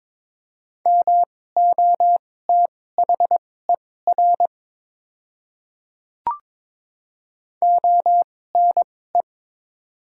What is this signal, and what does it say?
Morse code, heard as a keyed tone, about 21 wpm, spaced as for 16 wpm, 700 Hz. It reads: MOTHER ONE
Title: 2x Character Spacing - Top 300 Words - 22wpm